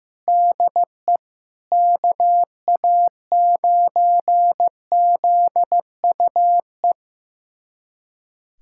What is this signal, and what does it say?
Morse code, heard as a keyed tone, about 16 wpm, 700 Hz. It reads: DE KA9ZUE
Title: USA Callsign DE Callsign 15wpm